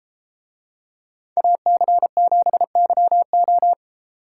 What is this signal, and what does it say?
Morse code, heard as a keyed tone, about 32 wpm, 700 Hz. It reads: AC7YO